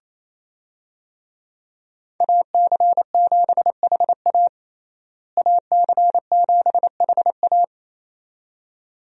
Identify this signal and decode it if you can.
Morse code, heard as a keyed tone, about 28 wpm, 700 Hz. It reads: AC7HA AC7HA
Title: USA Call Signs - 5 Characters - Repeated 2x - 28wpm